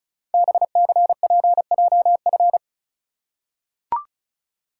Morse code, heard as a keyed tone, about 34 wpm, 690 Hz. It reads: BCPJF
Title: Code Groups - 5 Letters - 35wpm